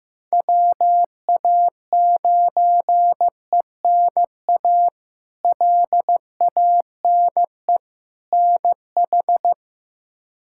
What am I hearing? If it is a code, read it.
Morse code, heard as a keyed tone, about 15 wpm, 700 Hz. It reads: WA9ENA LANE NH
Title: K1USN SST QSO Practice - 15wpm